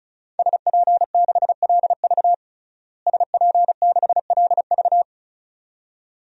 Morse code, heard as a keyed tone, about 34 wpm, 700 Hz. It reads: SP6LV SP6LV
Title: DXCC-100 Call Signs - 5 Characters - Repeated 2x - 35wpm